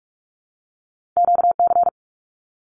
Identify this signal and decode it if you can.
Morse code, heard as a keyed tone, about 46 wpm, 700 Hz. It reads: QC
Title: Canadian Provinces and Territories 45wpm